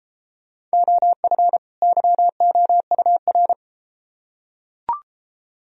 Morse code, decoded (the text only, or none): OF YOUR